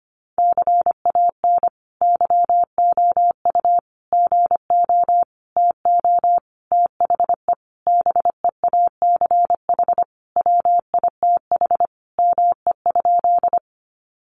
Morse code, heard as a keyed tone, about 25 wpm, 700 Hz. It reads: CAN YOU GO TO THE BEACH WITH ME?
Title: Sentences from Top 1500 Words 25wpm